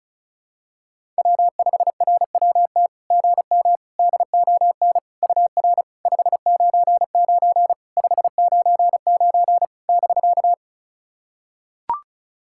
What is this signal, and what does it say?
Morse code, heard as a keyed tone, about 36 wpm, 660 Hz. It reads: W5RWT GM DON UR 599 599 <BK>
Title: SOTA QSO Practice 35wpm